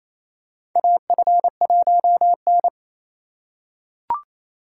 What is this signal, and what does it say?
Morse code, heard as a keyed tone, about 27 wpm, 700 Hz. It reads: AF1N
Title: USA Call Signs - 3x Character Spacing - 28wpm